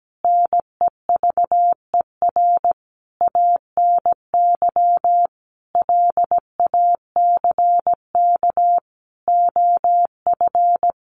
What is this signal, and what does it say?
Morse code, heard as a keyed tone, about 17 wpm, 700 Hz. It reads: NEVER ANY LACK OF
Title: Sentences from Top 1100 Words 17wpm